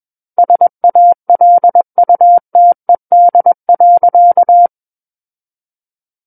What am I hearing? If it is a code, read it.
Morse code, heard as a keyed tone, about 21 wpm, 700 Hz. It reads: SALUTED.
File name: Book One / 12-Chapter 12 - 21 WPM